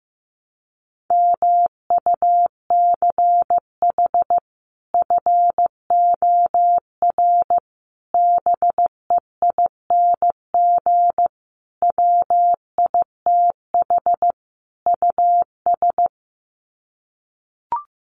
Morse code, heard as a keyed tone, about 15 wpm, 700 Hz. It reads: MUCH FOR BEING WITH US
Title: Sets of 5 Words 15wpm